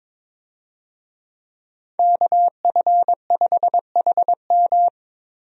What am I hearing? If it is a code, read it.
Morse code, heard as a keyed tone, about 22 wpm, 700 Hz. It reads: KF5HM
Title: USA Call Signs - 5 Characters - Repeated 3x - 22wpm